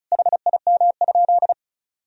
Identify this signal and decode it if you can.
Morse code, heard as a keyed tone, about 36 wpm, 700 Hz. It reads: HIM?
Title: Sentences from Top 100 Words and Abbreviations 35wpm